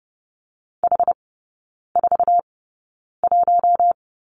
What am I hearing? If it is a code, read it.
Morse code, heard as a keyed tone, about 30 wpm, spaced as for 9 wpm, 690 Hz. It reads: H41
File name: Lesson 16e - Letter c - Three Characters 30wpm